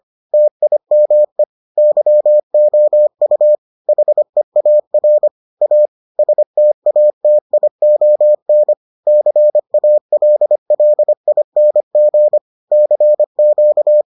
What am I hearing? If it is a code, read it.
Morse code, heard as a keyed tone, about 25 wpm, 600 Hz. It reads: TIME YOU HEAR A STATION CALLING CQ